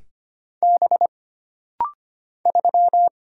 Morse code, heard as a keyed tone, about 25 wpm, 700 Hz. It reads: B 3
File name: Single Character (Letter:Number:Punctuation:Prosign) - Rapid-Fire - 25wpm